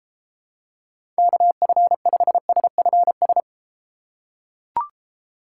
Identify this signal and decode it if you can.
Morse code, heard as a keyed tone, about 31 wpm, 700 Hz. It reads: KF5SFS